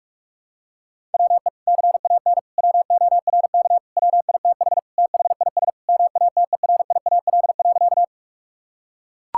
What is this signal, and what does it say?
Morse code, heard as a keyed tone, about 41 wpm, 700 Hz. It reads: WE CAN WORK WITH THIS MATERIAL.
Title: Sentences from Top 1000 Words - Review Part 5 - 45wpm